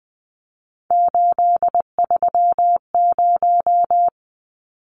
Morse code, heard as a keyed tone, about 20 wpm, 700 Hz. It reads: 830
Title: Numbers - 3-Digit 20wpm